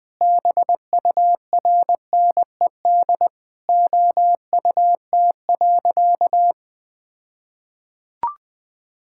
Morse code, heard as a keyed tone, about 21 wpm, 700 Hz. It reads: BURNED OUT.
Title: All-Age Appropriate Jokes - 20wpm